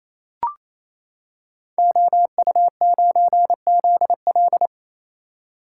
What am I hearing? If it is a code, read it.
Morse code, heard as a keyed tone, about 28 wpm, 700 Hz. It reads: OU9ZL
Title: Code Groups - 5 Letter:Numbers - 28wpm